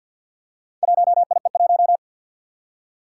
Morse code, heard as a keyed tone, about 53 wpm, 700 Hz. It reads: 1IE1
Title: Four Challenging Letter-Number Combinations 50wpm